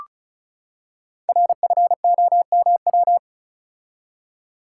Morse code, heard as a keyed tone, about 36 wpm, 700 Hz. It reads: RFOMW
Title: Code Groups - 5 Letters - 35wpm